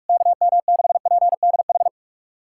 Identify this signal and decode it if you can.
Morse code, heard as a keyed tone, about 46 wpm, 700 Hz. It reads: KM6PDH
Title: USA Callsign DE Callsign - 6 Characters - 45wpm